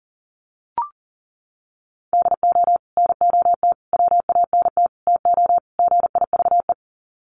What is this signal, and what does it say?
Morse code, heard as a keyed tone, about 40 wpm, 700 Hz. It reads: DO NOT WANT TO GIVE
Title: Sets of 5 Words Encore 40wpm